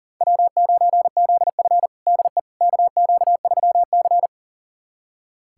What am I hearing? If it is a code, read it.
Morse code, heard as a keyed tone, about 37 wpm, 700 Hz. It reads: W9ZF DE KQ3C
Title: USA Callsign DE Callsign - 4 Characters - 40wpm